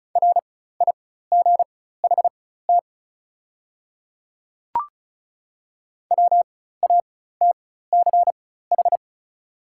Morse code, heard as a keyed tone, about 36 wpm, spaced as for 16 wpm, 690 Hz. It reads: RIGHT WATCH
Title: 4x Character Spacing - Top 300 Words - 35wpm